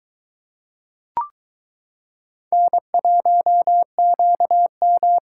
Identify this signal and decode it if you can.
Morse code, heard as a keyed tone, about 22 wpm, 700 Hz. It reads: N1QM